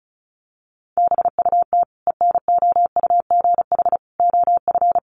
Morse code, heard as a keyed tone, about 34 wpm, 700 Hz. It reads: BUT ENOUGH OF
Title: Sentences from Top 400 Words 35wpm